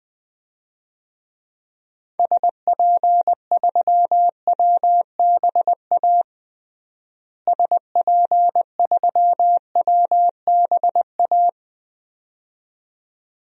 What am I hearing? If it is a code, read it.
Morse code, heard as a keyed tone, about 20 wpm, 700 Hz. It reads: SP3WBA SP3WBA
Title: DXCC-100 Call Signs - 6 Characters - Repeated 2x - 20wpm